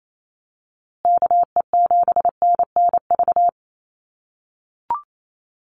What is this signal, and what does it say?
Morse code, heard as a keyed tone, about 28 wpm, 700 Hz. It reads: KE7NNV
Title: USA Call Signs Repeated 2x 28wpm